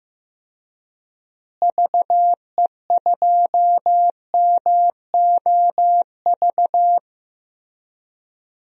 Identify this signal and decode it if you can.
Morse code, heard as a keyed tone, about 15 wpm, 700 Hz. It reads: VE2MOV
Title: DXCC-100 Call Signs - 6 Characters - Repeated 3x - 15wpm